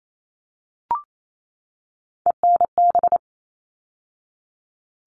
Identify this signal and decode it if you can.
Morse code, heard as a keyed tone, about 28 wpm, 700 Hz. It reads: ENB
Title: All Cut Numbers - 3-Digit 28wpm